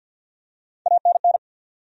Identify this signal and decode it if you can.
Morse code, heard as a keyed tone, about 49 wpm, 700 Hz. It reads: ANN
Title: USA Common Names - Max 3 Letters - 50wpm